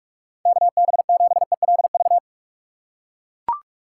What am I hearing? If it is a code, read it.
Morse code, heard as a keyed tone, about 42 wpm, 700 Hz. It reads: KB7ELV
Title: USA Call Signs - 6 Characters - 4x Character Spacing - 45wpm